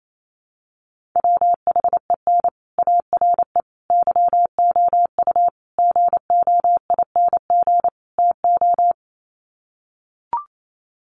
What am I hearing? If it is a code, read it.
Morse code, heard as a keyed tone, about 28 wpm, 700 Hz. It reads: WHEN ARE YOU GOING TO
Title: Sets of 5 Words Encore2 28wpm